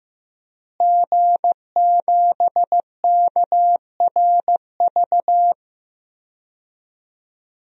Morse code, heard as a keyed tone, about 15 wpm, 700 Hz. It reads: G7KRV
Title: DXCC-100 Call Signs - 5 Characters - Repeated 2x - 15wpm